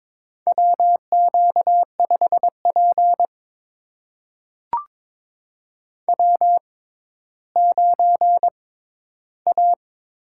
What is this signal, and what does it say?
Morse code, heard as a keyed tone, about 22 wpm, spaced as for 13 wpm, 700 Hz. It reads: WQ5P W9A
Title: USA Call Signs - 4 Characters - 6x Character Spacing - 22wpm